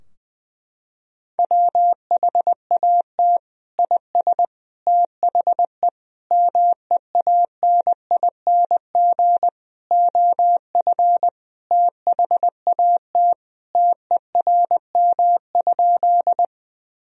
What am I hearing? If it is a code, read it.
Morse code, heard as a keyed tone, about 20 wpm, 700 Hz. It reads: WHAT IS THE MEANING OF THAT TERM?
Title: Sentences from Top 1000 Words - Review Part 4 - 20wpm